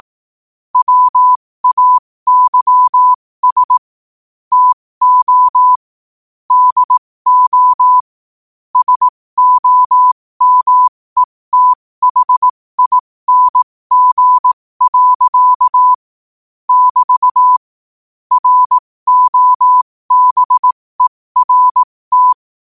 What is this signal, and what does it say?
Morse code, heard as a keyed tone, about 18 wpm, spaced as for 15 wpm, 1000 Hz. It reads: WAYS TO DO SOMETHING. <BT> ROBERT
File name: qotdfiles / 15wpm